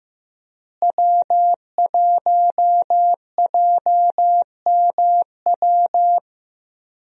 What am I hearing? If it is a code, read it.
Morse code, heard as a keyed tone, about 15 wpm, 700 Hz. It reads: W1JMW
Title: USA Call Signs - 5 Characters - Repeated 3x - 15wpm